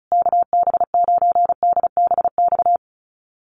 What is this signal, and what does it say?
Morse code, heard as a keyed tone, about 34 wpm, 700 Hz. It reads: KB9DBX
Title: USA Call Signs Repeated 3x 35wpm